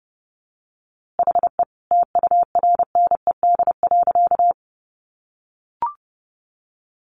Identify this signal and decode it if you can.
Morse code, heard as a keyed tone, about 30 wpm, 700 Hz. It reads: HE TURNED.
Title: A Sound Of Thunder 30wpm